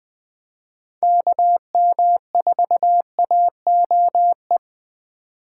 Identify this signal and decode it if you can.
Morse code, heard as a keyed tone, about 20 wpm, 700 Hz. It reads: KM4AOE
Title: Speed Racing - USA 2x3 Call Signs 17wpm